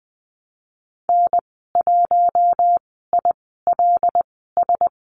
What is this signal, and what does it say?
Morse code, heard as a keyed tone, about 20 wpm, spaced as for 15 wpm, 700 Hz. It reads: N1ILS
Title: 082 Lesson 14g - Number 2 - Call Signs 20wpm